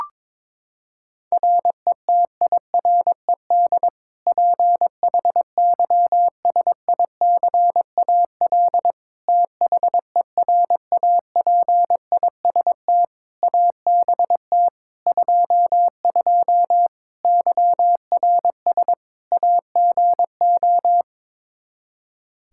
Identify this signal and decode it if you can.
Morse code, heard as a keyed tone, about 22 wpm, 700 Hz. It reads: RETIRED PHYSICAL THERAPIST ABT 22 YRS AGO
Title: Practice QSO - Exchange 3 - More Info - 22wpm